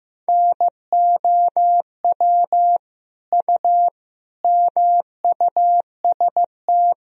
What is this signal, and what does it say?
Morse code, heard as a keyed tone, about 15 wpm, 700 Hz. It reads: NOW U MUST
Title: Sentences from Top 300 Words and Abbreviations 15wpm